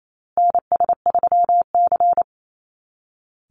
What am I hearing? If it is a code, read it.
Morse code, heard as a keyed tone, about 28 wpm, 700 Hz. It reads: NS3C
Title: Call Signs Repeated 2x - 28wpm